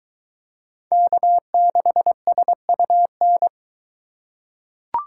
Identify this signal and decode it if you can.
Morse code, heard as a keyed tone, about 22 wpm, 700 Hz. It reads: K6SUN